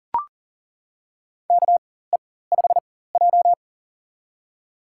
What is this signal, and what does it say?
Morse code, heard as a keyed tone, about 38 wpm, spaced as for 18 wpm, 700 Hz. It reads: KE5J
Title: USA Call Signs - 4x Character Spacing - 40wpm